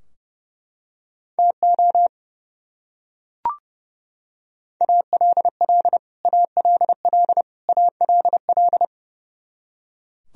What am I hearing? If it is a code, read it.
Morse code, heard as a keyed tone, about 30 wpm, 700 Hz. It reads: TO ALL ALL ALL
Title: Top 100 Words Repeated 30wpm